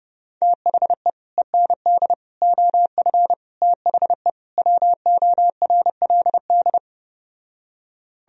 Morse code, heard as a keyed tone, about 30 wpm, 700 Hz. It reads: THE END OF THE WORLD